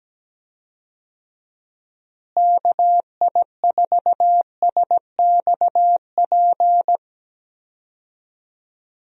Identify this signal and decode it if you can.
Morse code, heard as a keyed tone, about 17 wpm, 700 Hz. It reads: KI4SXP
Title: USA Call Signs - 6 Characters - Repeated 3x - 17wpm